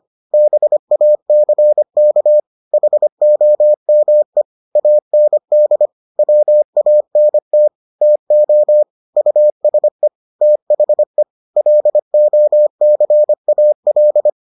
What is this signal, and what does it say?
Morse code, heard as a keyed tone, about 25 wpm, 600 Hz. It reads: BACK HOME AND WANT TO USE THE LOCAL